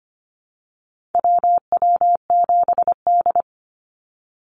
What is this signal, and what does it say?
Morse code, heard as a keyed tone, about 25 wpm, 690 Hz. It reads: WW7D
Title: NA SOTA Activator Callsigns 25wpm